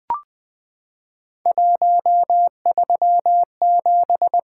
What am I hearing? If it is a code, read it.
Morse code, heard as a keyed tone, about 20 wpm, 700 Hz. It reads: 137
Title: Three Letter-Number Combinations - 20wpm